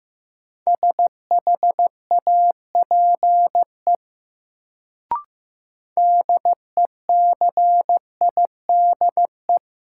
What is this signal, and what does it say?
Morse code, heard as a keyed tone, about 15 wpm, 700 Hz. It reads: SHAPE DECIDE
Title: Sentences from Top 500 Words - New Word Drill - 15wpm